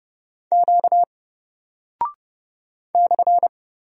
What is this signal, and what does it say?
Morse code, heard as a keyed tone, about 30 wpm, 700 Hz. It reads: Q /
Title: Single Character (Letter:Number:Punctuation:Prosign) 30wpm